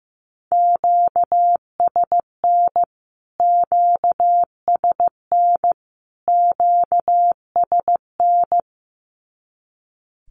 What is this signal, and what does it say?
Morse code, heard as a keyed tone, about 15 wpm, 700 Hz. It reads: QSN QSN QSN
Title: Q-Codes Repeated 3x 15wpm